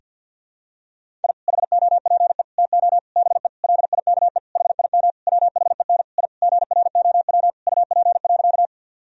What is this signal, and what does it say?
Morse code, heard as a keyed tone, about 50 wpm, 700 Hz. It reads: I HOPE TO BE LIKE HIM WHEN I GROW UP.
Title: Sentences from Top 1000 Words - Review Part 1 - 50wpm